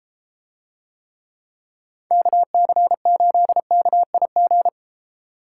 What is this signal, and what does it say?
Morse code, heard as a keyed tone, about 32 wpm, 700 Hz. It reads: KC8KIG